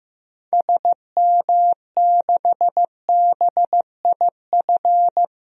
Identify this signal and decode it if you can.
Morse code, heard as a keyed tone, about 16 wpm, 700 Hz. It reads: SM6BIF
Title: DXCC-100 Call Signs - 6 Characters - 15wpm